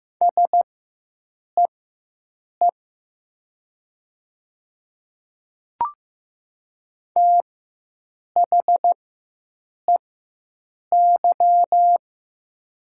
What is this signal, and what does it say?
Morse code, heard as a keyed tone, about 15 wpm, spaced as for 7 wpm, 700 Hz. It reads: SEE THEY
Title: 4x Character Spacing - Top 100 Words - 15wpm